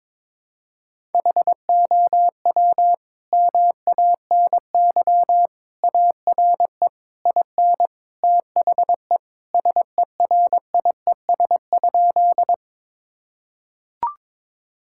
Morse code, heard as a keyed tone, about 22 wpm, 700 Hz. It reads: HOW MANY ARE IN THE SERIES?
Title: Sentences from Top 1000 Words - Review Part 5 - 22wpm